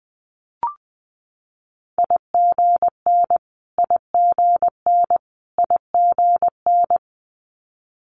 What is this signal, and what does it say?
Morse code, heard as a keyed tone, about 20 wpm, 700 Hz. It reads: IGN IGN IGN
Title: Multi-letter Phonemes Repeated 3x 20wpm